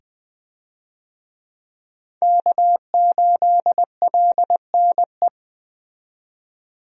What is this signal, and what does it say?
Morse code, heard as a keyed tone, about 21 wpm, 700 Hz. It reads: K8LNE
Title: USA Call Signs - 5 Characters - 20wpm